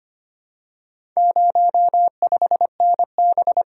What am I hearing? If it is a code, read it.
Morse code, heard as a keyed tone, about 25 wpm, 700 Hz. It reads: 05NB
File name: practice / 25wpm / Four Letter-Number Combinations - 25wpm